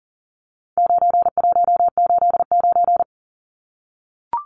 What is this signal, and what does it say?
Morse code, heard as a keyed tone, about 40 wpm, 700 Hz. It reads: 9189
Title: Numbers - 4-Digit 40wpm